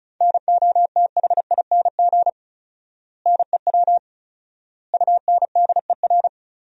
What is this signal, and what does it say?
Morse code, heard as a keyed tone, about 36 wpm, 700 Hz. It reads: NOTHING NEW UNDER
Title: 4x Word-Spacing - Sentences from Top 500 Words - 35wpm